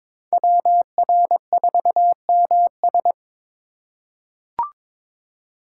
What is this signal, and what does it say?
Morse code, heard as a keyed tone, about 22 wpm, 700 Hz. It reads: WR4MS
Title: USA Call Signs - 5 Characters - 4x Character Spacing - 22wpm